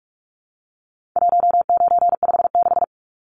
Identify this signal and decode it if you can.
Morse code, heard as a keyed tone, about 44 wpm, 700 Hz. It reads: 1956
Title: Numbers - 4-Digit 45wpm